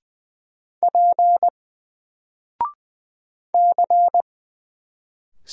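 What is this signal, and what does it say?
Morse code, heard as a keyed tone, about 20 wpm, 700 Hz. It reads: P C
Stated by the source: Single letter and number Morse drill at 20 wpm, from N4LCD.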